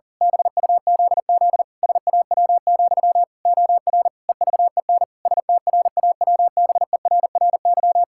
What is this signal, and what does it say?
Morse code, heard as a keyed tone, about 40 wpm, 700 Hz. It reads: BUZZ SAW, OR EVEN STRAWBERRY
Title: Pangram Challenge v2 - 40wpm